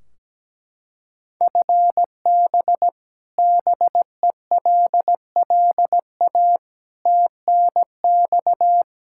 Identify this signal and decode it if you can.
Morse code, heard as a keyed tone, about 17 wpm, 700 Hz. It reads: FB BELLA TNX
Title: Practice QSO - Exchange 3 - More Info - 17wpm